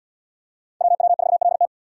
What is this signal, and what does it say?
Morse code, heard as a keyed tone, about 74 wpm, 700 Hz. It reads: WW4LT